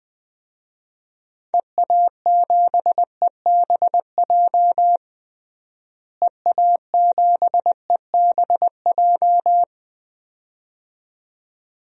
Morse code, heard as a keyed tone, about 20 wpm, 700 Hz. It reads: EA7EBJ EA7EBJ
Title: DXCC-100 Call Signs - 6 Characters - Repeated 2x - 20wpm